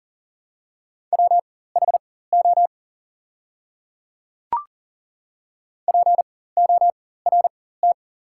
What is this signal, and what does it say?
Morse code, heard as a keyed tone, about 40 wpm, spaced as for 19 wpm, 690 Hz. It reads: WHO PORT
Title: 4x Character Spacing - Top 200 Words - 40wpm